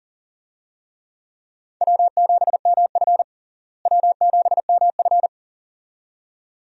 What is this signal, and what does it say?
Morse code, heard as a keyed tone, about 37 wpm, 700 Hz. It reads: W7MF W7MF
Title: USA Call Signs - 4 Characters - Repeated 2x - 40wpm